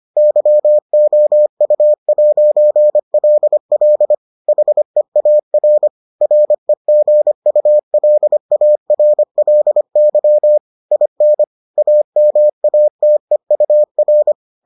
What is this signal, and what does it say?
Morse code, heard as a keyed tone, about 25 wpm, 600 Hz. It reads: YOU'LL HEAR REGULARLY IN AMATEUR